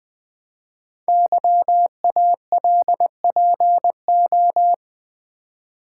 Morse code, heard as a keyed tone, about 20 wpm, 700 Hz. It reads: YALPO
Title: Code Groups - 5 Letters - 20wpm